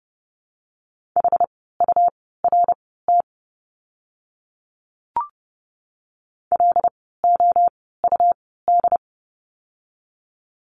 Morse code, heard as a keyed tone, about 30 wpm, spaced as for 17 wpm, 700 Hz. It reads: HURT LOUD
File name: Lesson 17f - Letter u - Words 30wpm